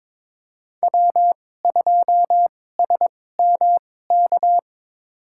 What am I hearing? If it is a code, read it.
Morse code, heard as a keyed tone, about 22 wpm, spaced as for 16 wpm, 700 Hz. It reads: W2SMK
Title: USA Call Signs - 5 Characters - 2x Character Spacing - 22wpm